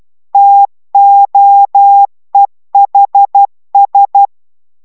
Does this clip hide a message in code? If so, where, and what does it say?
Yes: Morse code from 0.3 to 4.3 s, heard as a keyed tone, about 12 wpm, 800 Hz. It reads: TOEHS